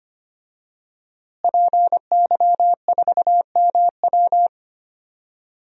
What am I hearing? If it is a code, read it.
Morse code, heard as a keyed tone, about 25 wpm, 700 Hz. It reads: PY4MW
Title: DXCC-100 Call Signs - 5 Characters - Repeated 3x - 25wpm